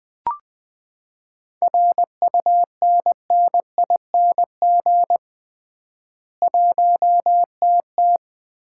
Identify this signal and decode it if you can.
Morse code, heard as a keyed tone, about 21 wpm, 700 Hz. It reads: RUNNING 1TT
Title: Common QSO Phrases - 3x Word-Spacing - 20wpm